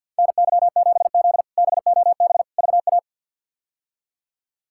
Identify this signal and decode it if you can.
Morse code, heard as a keyed tone, about 52 wpm, 700 Hz. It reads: NY7Z BOB VA
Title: K1USN SST QSO Practice - 50wpm